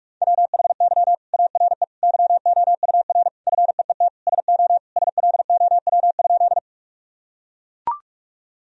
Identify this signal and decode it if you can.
Morse code, heard as a keyed tone, about 41 wpm, 700 Hz. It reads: WHY ARE YOUR FEET SO SLOW?
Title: Sentences from Top 1000 Words - Review Part 2 - 45wpm